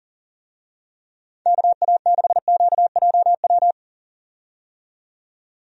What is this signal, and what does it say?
Morse code, heard as a keyed tone, about 40 wpm, 700 Hz. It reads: KA6QJW
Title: USA Call Signs - 6 Characters - 40wpm